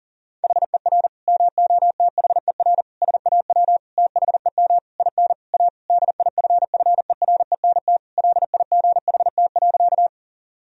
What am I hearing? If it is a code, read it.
Morse code, heard as a keyed tone, about 40 wpm, 700 Hz. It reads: HER MOTHER SAW THEM IN A DIFFERENT LIGHT.
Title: Sentences from Top 1000 Words - Review Part 3 - 40wpm